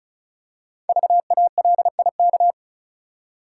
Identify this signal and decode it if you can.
Morse code, heard as a keyed tone, about 36 wpm, 690 Hz. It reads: VALIK
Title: Code Groups - 5 Letters - 35wpm